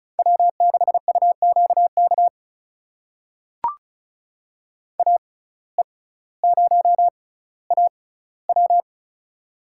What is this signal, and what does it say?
Morse code, heard as a keyed tone, about 36 wpm, spaced as for 18 wpm, 700 Hz. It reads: W6UQK AE0AW
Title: USA Call Signs - 5 Characters - 6x Character Spacing - 35wpm